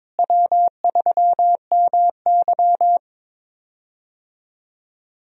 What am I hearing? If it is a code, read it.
Morse code, heard as a keyed tone, about 22 wpm, 700 Hz. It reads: W3MY
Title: USA Call Signs - 4 Characters - Repeated 3x - 22wpm